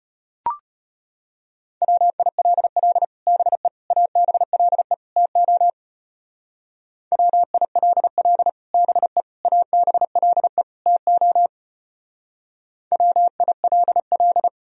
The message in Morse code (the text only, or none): WILL BE ABLE TO WILL BE ABLE TO WILL